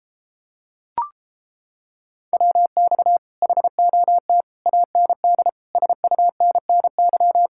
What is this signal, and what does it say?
Morse code, heard as a keyed tone, about 31 wpm, 700 Hz. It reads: WX HOT AND SUNNY